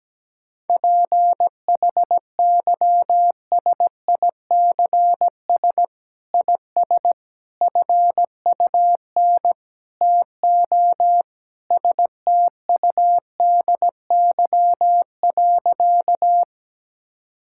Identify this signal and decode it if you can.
Morse code, heard as a keyed tone, about 17 wpm, 700 Hz. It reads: PHYSICS IS FUN TO STUDY.
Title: Sentences from Top 1000 Words - Review Part 5 - 17wpm